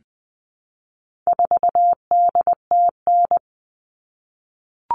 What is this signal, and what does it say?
Morse code, heard as a keyed tone, about 20 wpm, 700 Hz. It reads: 4DTN
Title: All Cut Numbers - 4-Digit 20wpm